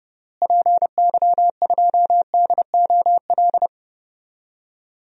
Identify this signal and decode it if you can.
Morse code, heard as a keyed tone, about 30 wpm, 700 Hz. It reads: PY2DOL
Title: DXCC-100 Call Signs - 6 Characters - Repeated 2x - 30wpm